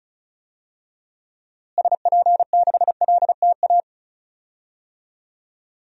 Morse code, heard as a keyed tone, about 34 wpm, 700 Hz. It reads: SP6LTA
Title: DXCC-100 Call Signs - 6 Characters - Repeated 3x - 35wpm